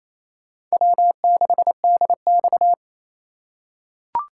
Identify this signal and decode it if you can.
Morse code, heard as a keyed tone, about 28 wpm, 700 Hz. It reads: W6DX
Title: USA Call Signs - 3x Character Spacing - 28wpm